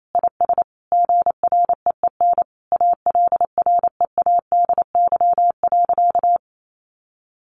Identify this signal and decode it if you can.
Morse code, heard as a keyed tone, about 28 wpm, 700 Hz. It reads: IS GREEN ALREADY.
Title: Sentences from Top 700 Words 28wpm